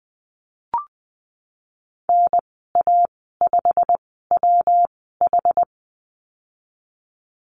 Morse code, heard as a keyed tone, about 20 wpm, spaced as for 15 wpm, 700 Hz. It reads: NA5WH
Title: 122 Lesson 20g - Number 3 - Call Signs 20wpm